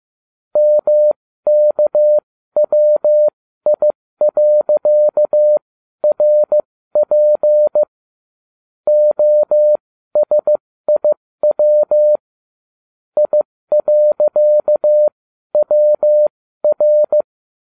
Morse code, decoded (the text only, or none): MKWI.RP OSIW I.WR